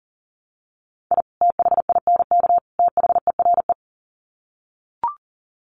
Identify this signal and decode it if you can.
Morse code, heard as a keyed tone, about 40 wpm, 700 Hz. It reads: I THINK THERE
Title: Sets of 3 Words 40wpm